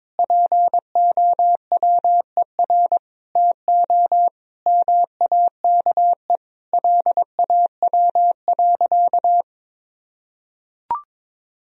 Morse code, heard as a keyed tone, about 22 wpm, 700 Hz. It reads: POWER TO MAKE LAW.
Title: Sentences from Top 1000 Words - Review Part 3 - 22wpm